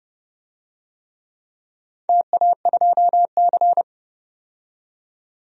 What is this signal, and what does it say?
Morse code, heard as a keyed tone, about 31 wpm, 700 Hz. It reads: TA2C
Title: DXCC-100 Call Signs - 4 Characters - 30wpm